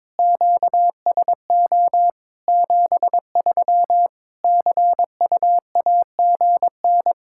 Morse code, heard as a keyed tone, about 22 wpm, 700 Hz. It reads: QSO 73 CUAGN
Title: Practice QSO - Exchange 4 - Ending - 22wpm